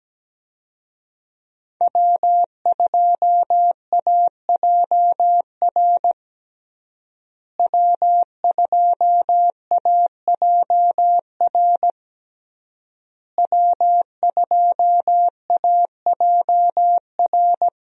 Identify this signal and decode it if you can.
Morse code, heard as a keyed tone, about 17 wpm, 700 Hz. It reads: W2AJR W2AJR W2AJR
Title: USA Call Signs - 5 Characters - Repeated 3x - 17wpm